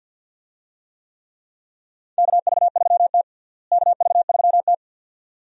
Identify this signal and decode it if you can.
Morse code, heard as a keyed tone, about 45 wpm, 700 Hz. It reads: KV3T KV3T
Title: USA Call Signs - 4 Characters - Repeated 2x - 50wpm